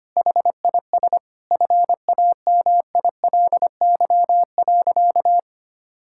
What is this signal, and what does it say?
Morse code, heard as a keyed tone, about 25 wpm, 700 Hz. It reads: HIS FAMILY.
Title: Sentences from Top 1000 Words - Review Part 5 - 25wpm